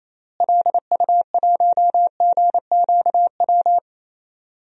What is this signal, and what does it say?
Morse code, heard as a keyed tone, about 28 wpm, 700 Hz. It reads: LU1GQW
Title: DXCC-100 Call Signs - 6 Characters - Repeated 2x - 28wpm